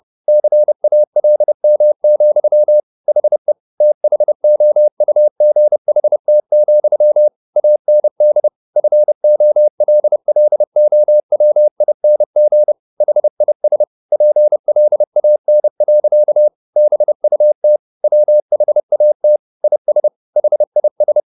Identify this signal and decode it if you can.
Morse code, heard as a keyed tone, about 30 wpm, 600 Hz. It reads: CALM, HE THOUGHT, AND FOLLOWING HIS PLAN. BUT WHAT IS HIS
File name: the old man and the sea / the old man and the sea 1 30wpm